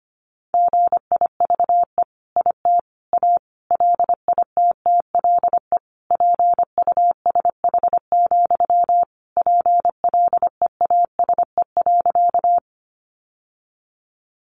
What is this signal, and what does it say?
Morse code, heard as a keyed tone, about 25 wpm, 700 Hz. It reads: GIVE IT A LITTLE PUSH, PLEASE.
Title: Sentences from Top 1500 Words 25wpm